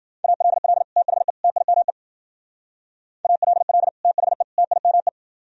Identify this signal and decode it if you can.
Morse code, heard as a keyed tone, about 50 wpm, 700 Hz. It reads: ALL THE TIME ALL THE TIME